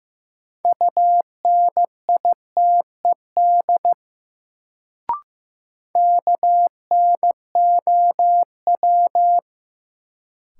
Morse code, heard as a keyed tone, about 15 wpm, 700 Hz. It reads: UNITED KNOW
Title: Sets of 2 Words - New Word Drill - 15wpm